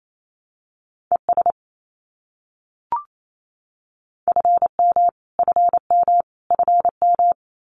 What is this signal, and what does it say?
Morse code, heard as a keyed tone, about 28 wpm, 700 Hz. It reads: ES FM FM FM
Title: Abbreviations Repeated 3x - 28wpm